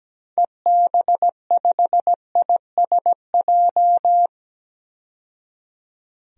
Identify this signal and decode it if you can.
Morse code, heard as a keyed tone, about 17 wpm, 700 Hz. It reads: EB5ISJ
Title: DXCC-100 Call Signs - 6 Characters - 17wpm